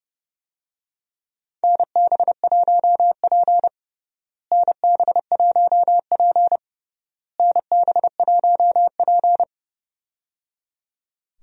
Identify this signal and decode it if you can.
Morse code, heard as a keyed tone, about 30 wpm, 700 Hz. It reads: NB1P NB1P NB1P
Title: USA Call Signs - 4 Characters - Repeated 3x - 30wpm